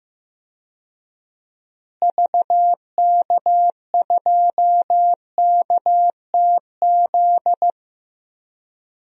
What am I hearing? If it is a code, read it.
Morse code, heard as a keyed tone, about 15 wpm, 700 Hz. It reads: VK2KTZ
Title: DXCC-100 Call Signs - 6 Characters - Repeated 2x - 15wpm